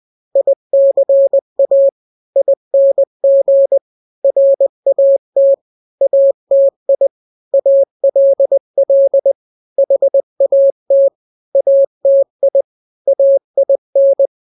Morse code, heard as a keyed tone, about 20 wpm, 550 Hz. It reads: ICA ING RAT ATI ALL HAT ATI AIN
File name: n-grams / 3-grams 20 18 WPM